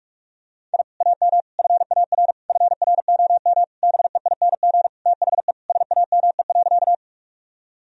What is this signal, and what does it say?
Morse code, heard as a keyed tone, about 46 wpm, 700 Hz. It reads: I AM FAR FROM BEING THE SAME.
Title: Sentences from Top 1000 Words - Review Part 3 - 45wpm